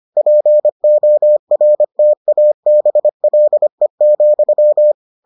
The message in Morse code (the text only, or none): PORTABLE,